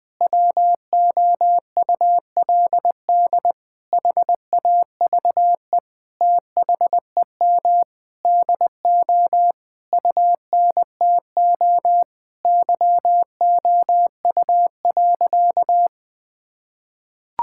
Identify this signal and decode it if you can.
Morse code, heard as a keyed tone, about 20 wpm, 700 Hz. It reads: WOULD HAVE THEM DO UNTO YOU.
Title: Sentences with 9-Word Phrases - 20wpm